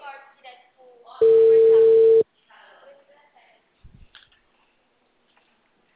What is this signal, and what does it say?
Ringback tone (China, 450 Hz), with speech.